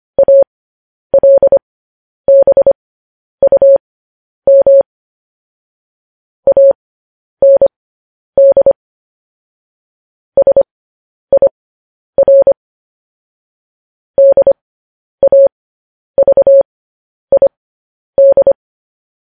Morse code, heard as a keyed tone, about 25 wpm, spaced as for 10 wpm, 550 Hz. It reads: ALBUM AND SIR DAVID
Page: news-headlines-in-morse-code-at-10-wpm